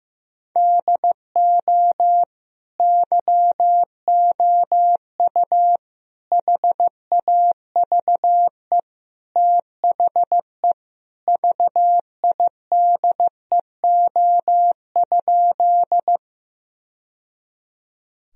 Morse code, heard as a keyed tone, about 16 wpm, 700 Hz. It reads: DO YOU HAVE THE VIDEO?
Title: Sentences from Top 1000 Words - Review Part 4 - 15wpm